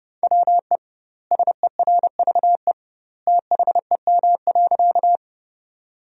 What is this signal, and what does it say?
Morse code, heard as a keyed tone, about 30 wpm, 700 Hz. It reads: WE SERVE THEM.
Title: 2x Word-Spacing - Sentences from Top 400 Words - 30wpm